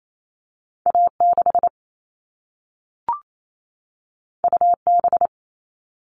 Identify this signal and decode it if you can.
Morse code, heard as a keyed tone, about 28 wpm, 700 Hz. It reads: A6 UB
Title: All Cut Numbers - 2-Digit 28wpm